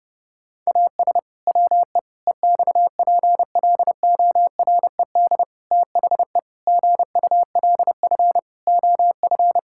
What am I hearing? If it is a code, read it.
Morse code, heard as a keyed tone, about 30 wpm, 700 Hz. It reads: AS WE EXPLORED THE GULF OF
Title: Pangram Challenge v2 - 30wpm